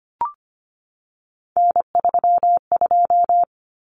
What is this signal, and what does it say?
Morse code, heard as a keyed tone, about 25 wpm, 700 Hz. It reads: N32
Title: Common Cut Numbers - 3-Digit 25wpm